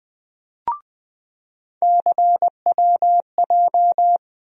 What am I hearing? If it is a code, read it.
Morse code, heard as a keyed tone, about 20 wpm, 700 Hz. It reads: CWJ
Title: USA Call Sign Postfixes - 3 Letters 20wpm